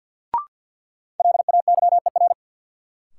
Morse code, heard as a keyed tone, about 52 wpm, 700 Hz. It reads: LAYER
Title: Sentences from Top 1600 Words - New Word Drill Alt - 50wpm